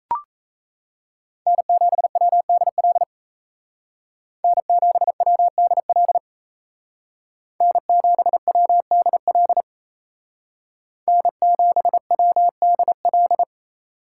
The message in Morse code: N7WDL N7WDL N7WDL N7WDL

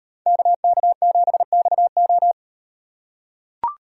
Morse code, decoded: KK7XO

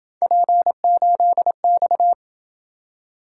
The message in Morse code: P8X

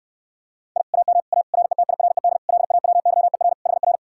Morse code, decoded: I AM A RETIRED LAWYER HR

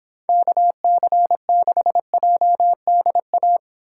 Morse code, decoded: KC6JDA

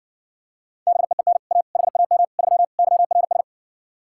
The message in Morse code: BEEN A HAM 3 YRS

